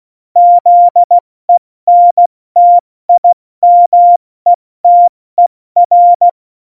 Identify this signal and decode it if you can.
Morse code, heard as a keyed tone, about 16 wpm, spaced as for 14 wpm, 700 Hz. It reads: ZENTIMETER
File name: Morsekurs / Witze / Witz 29. September 2020, 70 F80